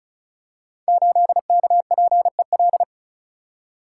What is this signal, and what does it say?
Morse code, heard as a keyed tone, about 36 wpm, 700 Hz. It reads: 8KPEL
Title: Code Groups - 5 Letter:Numbers - 35wpm